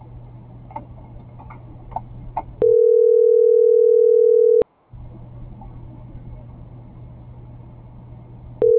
Ringback tone (North America), 440 + 480 Hz.
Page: /2025/09/16/CENTRAL SKYBLUE/